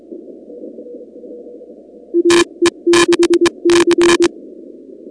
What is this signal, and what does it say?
Morse code, heard as a keyed tone, about 22 wpm, 340 Hz. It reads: AE6C
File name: contests / 2019-cw-wpx-kt5j-k5pi- -2330 0-37